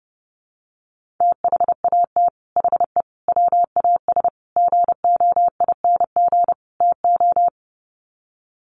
Morse code, heard as a keyed tone, about 30 wpm, 700 Hz. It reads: THAT HE WAS GOING TO
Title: Sets of 5 Words 30wpm